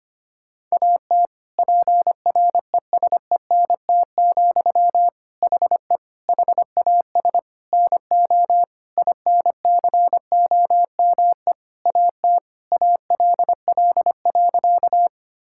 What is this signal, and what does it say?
Morse code, heard as a keyed tone, about 25 wpm, 700 Hz. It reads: AT PRESENT, HE HAS NO INCOME AT ALL.
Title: Sentences from Top 1000 Words - Review Part 5 - 25wpm